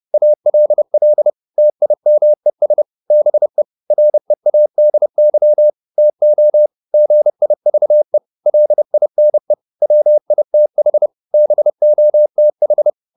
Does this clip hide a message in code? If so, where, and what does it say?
Yes: Morse code from 0.1 to 12.9 s, heard as a keyed tone, about 30 wpm, 600 Hz. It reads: ALL TIMES BE READY TO GIVE LINE WITH BOTH